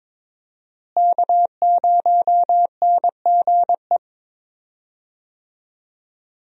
Morse code, heard as a keyed tone, about 22 wpm, 700 Hz. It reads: K0NGE
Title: USA Call Signs - 5 Characters - Repeated 2x - 22wpm